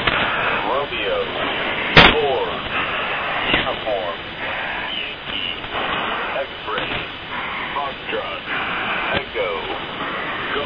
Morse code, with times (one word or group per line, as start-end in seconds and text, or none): none